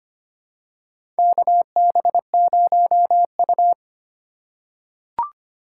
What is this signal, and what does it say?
Morse code, heard as a keyed tone, about 25 wpm, 700 Hz. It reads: KB0U
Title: USA Call Signs - 2x Character Spacing - 25wpm